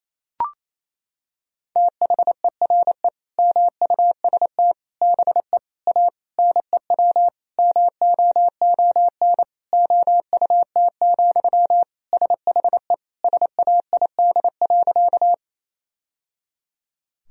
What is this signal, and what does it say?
Morse code, heard as a keyed tone, about 27 wpm, 700 Hz. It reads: THERE MUST BE A NEW MOON OUT, SHE SAID.
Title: Sentences from Top 1000 Words - Review Part 3 - 28wpm